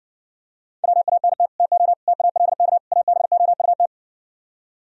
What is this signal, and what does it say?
Morse code, heard as a keyed tone, about 67 wpm, 700 Hz. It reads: WANT TO TALK ABOUT